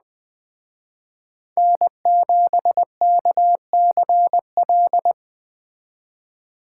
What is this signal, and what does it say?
Morse code, heard as a keyed tone, about 21 wpm, 700 Hz. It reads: N7KCL
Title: Speed Racing - USA 1x3 Call Signs 17wpm